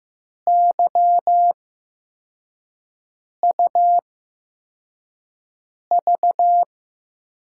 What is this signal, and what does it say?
Morse code, heard as a keyed tone, about 15 wpm, spaced as for 4 wpm, 700 Hz. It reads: YUV
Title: USA Call Signs - 5 Characters - 8x Character Spacing - 15wpm